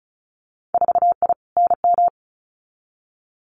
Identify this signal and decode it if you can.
Morse code, heard as a keyed tone, about 36 wpm, 700 Hz. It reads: 4I NM
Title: Winter Field Day Exchanges Encore 35wpm